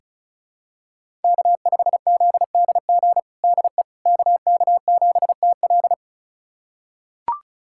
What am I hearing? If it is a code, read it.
Morse code, heard as a keyed tone, about 36 wpm, 700 Hz. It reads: K5ZDG DE KK7TL
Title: USA Callsign DE Callsign - 5 Characters - 35wpm